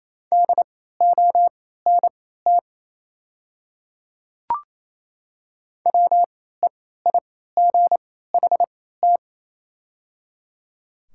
Morse code, heard as a keyed tone, about 28 wpm, spaced as for 16 wpm, 690 Hz. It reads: DONT WEIGHT
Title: 3x Character Spacing - Top 500 Words - 28wpm